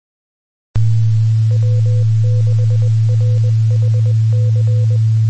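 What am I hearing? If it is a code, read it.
Morse code, heard as a keyed tone, about 20 wpm, 500 Hz, with noise. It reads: W6RHC
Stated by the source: Morse signal recorded on 2 m at 146.850 MHz.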